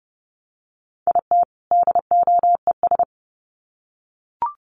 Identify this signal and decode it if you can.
Morse code, heard as a keyed tone, about 30 wpm, 700 Hz. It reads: IT DOES
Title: Sets of 2 Words 30wpm